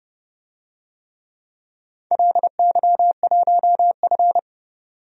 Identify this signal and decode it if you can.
Morse code, heard as a keyed tone, about 28 wpm, 700 Hz. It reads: LY1F
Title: DXCC-100 Call Signs - 4 Characters - 30wpm